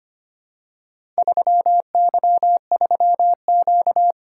Morse code, heard as a keyed tone, about 25 wpm, 700 Hz. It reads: 3Y3Q
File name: practice / 25wpm / Four Letter-Number Combinations - 25wpm